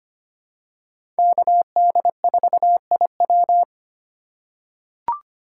Morse code, heard as a keyed tone, about 25 wpm, 700 Hz. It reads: KD4IW
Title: USA Call Signs - 5 Characters - 3x Character Spacing - 25wpm